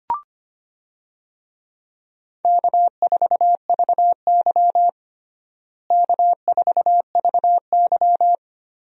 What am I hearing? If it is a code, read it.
Morse code, heard as a keyed tone, about 25 wpm, 700 Hz. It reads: K4VY K4VY
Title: USA Call Signs - 4 Characters - Repeated 2x - 25wpm